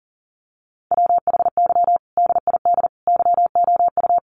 Morse code, heard as a keyed tone, about 40 wpm, 700 Hz. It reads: WHY DID YOU
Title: Sentences from Top 500 Words 40wpm